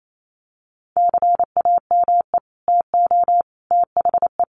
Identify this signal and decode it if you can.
Morse code, heard as a keyed tone, about 28 wpm, 700 Hz. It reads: CAME TO THE
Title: Sets of 5 Words Encore2 28wpm